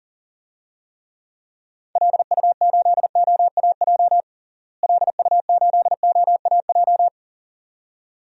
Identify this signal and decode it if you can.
Morse code, heard as a keyed tone, about 37 wpm, 700 Hz. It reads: LU8OAJ LU8OAJ
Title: DXCC-100 Call Signs - 6 Characters - Repeated 2x - 40wpm